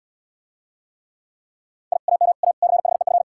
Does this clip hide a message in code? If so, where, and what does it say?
Yes: Morse code from 1.9 to 3.2 s, heard as a keyed tone, about 79 wpm, 700 Hz. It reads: I AM A CHEF